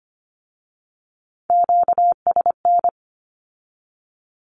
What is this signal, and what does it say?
Morse code, heard as a keyed tone, about 25 wpm, 700 Hz. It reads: QSN
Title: Q-Codes with Spoken Meaning - 25wpm